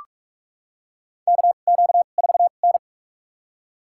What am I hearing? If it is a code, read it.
Morse code, heard as a keyed tone, about 44 wpm, spaced as for 32 wpm, 690 Hz. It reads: KQ4N
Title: USA Call Signs - 4 Characters - 2x Character Spacing - 45wpm